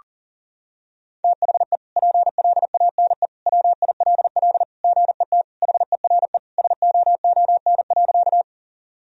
Morse code, heard as a keyed tone, about 37 wpm, 700 Hz. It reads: THE PLANE WILL GET HERE SOON.
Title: Sentences from Top 1000 Words - Review Part 3 - 40wpm